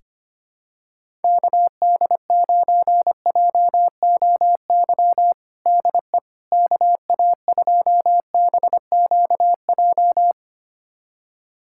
Morse code, heard as a keyed tone, about 25 wpm, 700 Hz. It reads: KD9JOY DE KA2BQJ
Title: USA Callsign DE Callsign - 6 Characters - 25wpm